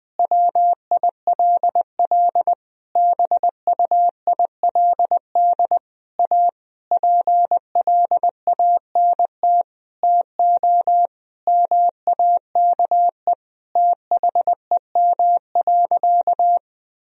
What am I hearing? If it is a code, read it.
Morse code, heard as a keyed tone, about 20 wpm, 700 Hz. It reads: WILL BUILD A PLANT TO MAKE THEM.
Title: Sentences from Top 1000 Words - Review Part 1 - 20wpm